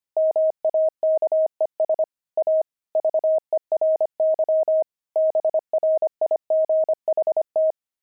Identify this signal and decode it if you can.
Morse code, heard as a keyed tone, about 25 wpm, 630 Hz. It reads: MAKES A VERY BRIGHT
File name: pangram-fs-25-cs-25-oxide